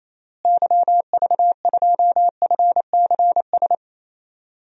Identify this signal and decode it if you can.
Morse code, heard as a keyed tone, about 28 wpm, 700 Hz. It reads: YV2FCS
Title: DXCC-100 Call Signs - 6 Characters - Repeated 2x - 28wpm